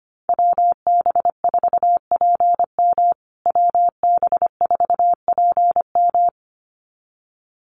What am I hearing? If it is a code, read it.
Morse code, heard as a keyed tone, about 25 wpm, 700 Hz. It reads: WB4PM WB4PM
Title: USA Call Signs Repeated 2x 25wpm